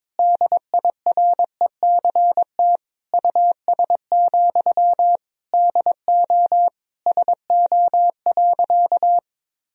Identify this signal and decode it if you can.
Morse code, heard as a keyed tone, about 22 wpm, 700 Hz. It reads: DIRECT US, DO SO.
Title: Sentences from Top 1000 Words - Review Part 2 - 22wpm